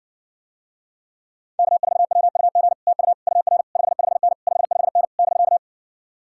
Morse code, heard as a keyed tone, about 60 wpm, 700 Hz. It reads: K4WUG TU UR 55N 55N <BK>